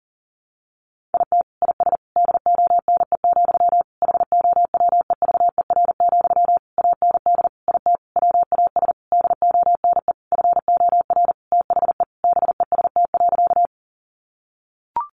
Morse code, heard as a keyed tone, about 40 wpm, 700 Hz. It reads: IT IS DONE, HOWEVER, AND IT WAS DONE FOR THE BEST.
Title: Sentences from Top 600 Words 40wpm